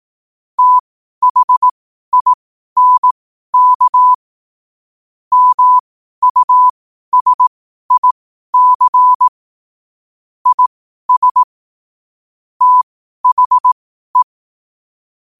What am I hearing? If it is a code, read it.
Morse code, heard as a keyed tone, about 18 wpm, spaced as for 12 wpm, 1000 Hz. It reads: THINK MUSIC IS THE
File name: qotdfiles / 12wpm